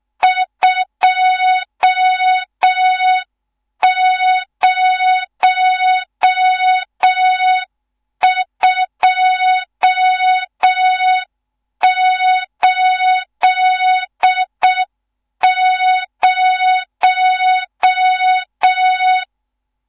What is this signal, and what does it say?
Morse code, heard as a keyed tone, about 6 wpm, 3700 Hz. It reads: 20280
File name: Morze-HA2MN / hang / 14 280 HA2MN